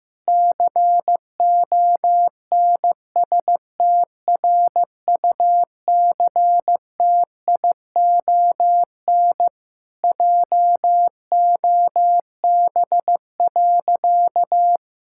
Morse code, heard as a keyed tone, about 15 wpm, 700 Hz. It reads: CONSTRUCTION JOB.
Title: Sentences from Top 1600 Words 15wpm